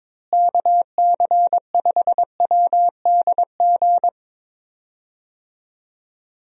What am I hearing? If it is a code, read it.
Morse code, heard as a keyed tone, about 22 wpm, 700 Hz. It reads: KC5WDG
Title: USA Call Signs - 6 Characters - Repeated 3x - 22wpm